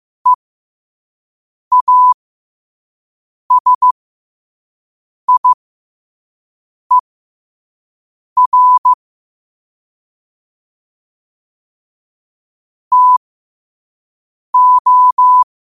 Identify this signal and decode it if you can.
Morse code, heard as a keyed tone, about 15 wpm, spaced as for 5 wpm, 1000 Hz. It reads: EASIER TO
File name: 5wpm